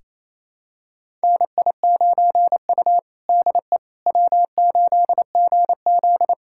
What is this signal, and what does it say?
Morse code, heard as a keyed tone, about 28 wpm, 700 Hz. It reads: NI9U DE W8GZ
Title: USA Callsign DE Callsign - 4 Characters - 28wpm